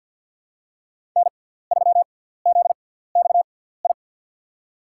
Morse code, heard as a keyed tone, about 53 wpm, spaced as for 17 wpm, 700 Hz. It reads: N3ZXI
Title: USA Call Signs - 5 Characters - 6x Character Spacing - 50wpm